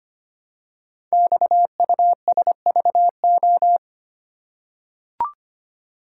Morse code, heard as a keyed tone, about 25 wpm, 700 Hz. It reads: XUSVO
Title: Code Groups - 5 Letter:Numbers - 25wpm